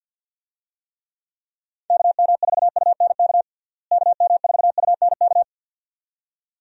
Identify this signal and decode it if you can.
Morse code, heard as a keyed tone, about 46 wpm, 700 Hz. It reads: KM4UNK KM4UNK
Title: USA Call Signs - 6 Characters - Repeated 2x - 50wpm